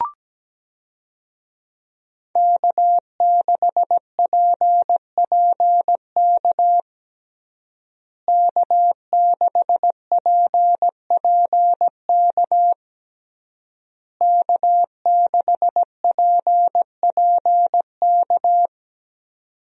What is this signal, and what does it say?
Morse code, heard as a keyed tone, about 17 wpm, 700 Hz. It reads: K6PPK K6PPK K6PPK
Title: USA Call Signs - 5 Characters - Repeated 3x - 17wpm